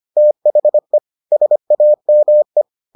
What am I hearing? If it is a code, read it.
Morse code, heard as a keyed tone, about 25 wpm, 600 Hz. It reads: THE SAME